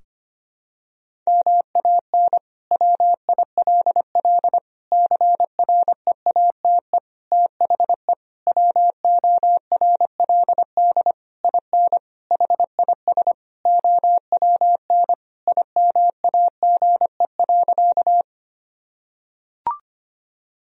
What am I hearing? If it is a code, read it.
Morse code, heard as a keyed tone, about 25 wpm, 700 Hz. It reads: MAN WILL CREATE THE WORLD IN HIS OWN IMAGE.
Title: Sentences from Top 1000 Words - Review Part 4 - 25wpm